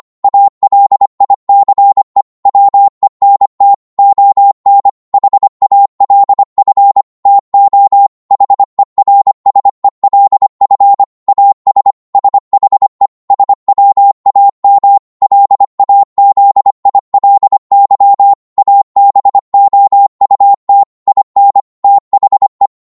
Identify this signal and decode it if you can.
Morse code, heard as a keyed tone, about 25 wpm, 800 Hz. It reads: ALICE WENT ON HALF TO HERSELF AS SHE SWAM LAZILY ABOUT IN THE